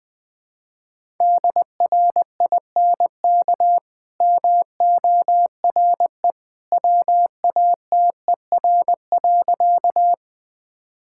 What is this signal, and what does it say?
Morse code, heard as a keyed tone, about 20 wpm, 700 Hz. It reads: DRINK MORE WATER.
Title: Easy Sentences with Hints - 20wpm